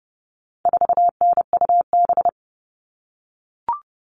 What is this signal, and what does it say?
Morse code, heard as a keyed tone, about 30 wpm, 700 Hz. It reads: 4NUB
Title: All Cut Numbers - 4-Digit 30wpm